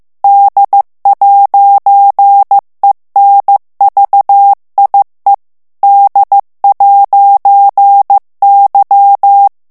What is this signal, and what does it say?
Morse code, heard as a keyed tone, about 15 wpm, 800 Hz. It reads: D'ENVIE D'Y
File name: morse mp3 / FRANCAIS TEXTE CLAIR 2 VITESSE 14 MOTS MINUTES